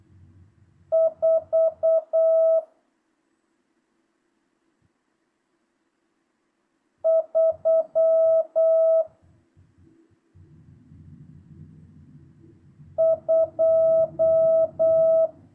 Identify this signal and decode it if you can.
Morse code, heard as a keyed tone, about 8 wpm, 650 Hz. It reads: 4 3 2
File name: jh8cbh / 2025cw